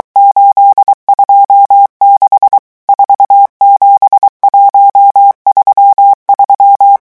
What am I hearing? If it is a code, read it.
Morse code, heard as a keyed tone, about 23 wpm, 800 Hz. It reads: 826 47133
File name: morse mp3 / revision11 chiffres vitesse 22